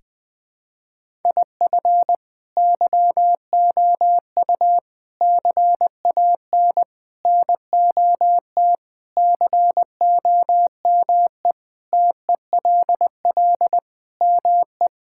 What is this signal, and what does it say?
Morse code, heard as a keyed tone, about 20 wpm, 700 Hz. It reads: IF YOU CAN NOT COME TELL ME
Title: Sentences with 4-Word Phrases - 20wpm